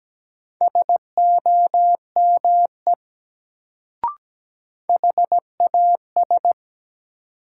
Morse code, heard as a keyed tone, about 17 wpm, 700 Hz. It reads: SOME HAS
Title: Sentences from Top 100 Words - New Word Drill - 17wpm